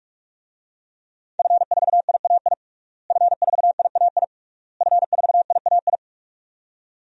Morse code, heard as a keyed tone, about 47 wpm, 700 Hz. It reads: F4IAI F4IAI F4IAI
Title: DXCC-100 Call Signs - 5 Characters - Repeated 3x - 45wpm